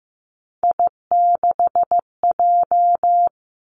Morse code, heard as a keyed tone, about 15 wpm, 700 Hz. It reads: I6J
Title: Three Challenging Letter-Number Combinations 15wpm